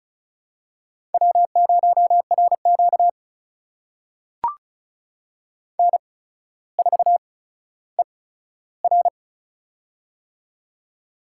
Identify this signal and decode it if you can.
Morse code, heard as a keyed tone, about 36 wpm, spaced as for 15 wpm, 700 Hz. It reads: W0RQ N4ER
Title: USA Call Signs - 4 Characters - 8x Character Spacing - 35wpm